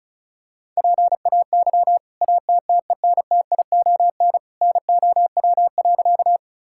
Morse code, heard as a keyed tone, about 36 wpm, 700 Hz. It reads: PAY ATTENTION NOW.
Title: Sentences from Top 1000 Words - Review Part 6 - 35wpm